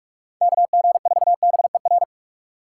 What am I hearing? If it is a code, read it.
Morse code, heard as a keyed tone, about 40 wpm, 700 Hz. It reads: KG4BER